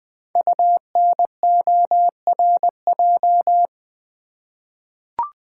Morse code, heard as a keyed tone, about 20 wpm, 700 Hz. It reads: UNORJ
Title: Code Groups - 5 Letters - 20wpm